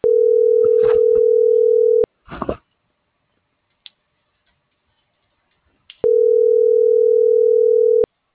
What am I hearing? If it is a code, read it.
Ringback tone (North America), 440 + 480 Hz.